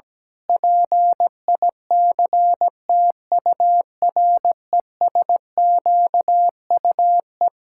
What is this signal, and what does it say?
Morse code, heard as a keyed tone, about 17 wpm, 700 Hz. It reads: PICTURESQUE
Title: Pangram Challenge v2 - 17wpm